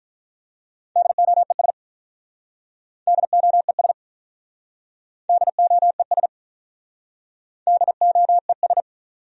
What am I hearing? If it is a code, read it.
Morse code, heard as a keyed tone, about 47 wpm, 700 Hz. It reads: DOES DOES DOES DOES